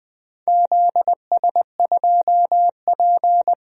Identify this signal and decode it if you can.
Morse code, heard as a keyed tone, about 20 wpm, 700 Hz. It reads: ZS2P
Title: DXCC-100 Call Signs - 4 Characters - Repeated 2x - 20wpm